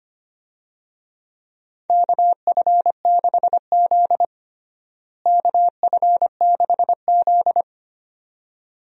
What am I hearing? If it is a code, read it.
Morse code, heard as a keyed tone, about 25 wpm, 700 Hz. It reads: KF6Z KF6Z
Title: USA Call Signs - 4 Characters - Repeated 2x - 25wpm